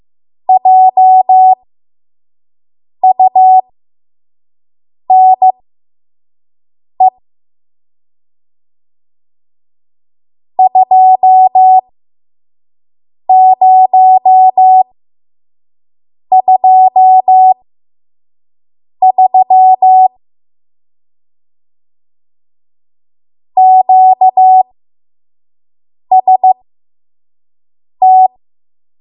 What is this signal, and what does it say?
Morse code, heard as a keyed tone, about 15 wpm, spaced as for 5 wpm, 750 Hz. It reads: JUNE 2023 QST
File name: Morse / 05WPM